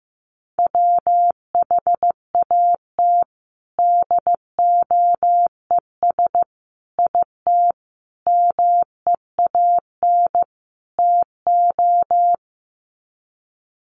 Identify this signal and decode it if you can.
Morse code, heard as a keyed tone, about 15 wpm, 700 Hz. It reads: WHAT DOES IT MEAN TO
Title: Sets of 5 Words Encore2 15wpm